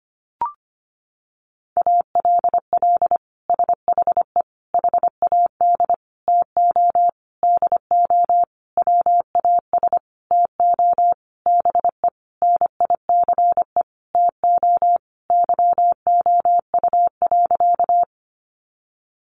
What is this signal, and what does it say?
Morse code, heard as a keyed tone, about 25 wpm, 700 Hz. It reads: ALL SHE HAD TO DO WAS TO BE NICE TO YOU.
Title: Sentences from Top 1100 Words 25wpm